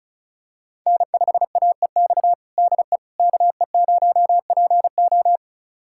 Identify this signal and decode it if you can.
Morse code, heard as a keyed tone, about 36 wpm, 700 Hz. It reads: N5AEX DE KE0PO
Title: USA Callsign DE Callsign - 5 Characters - 35wpm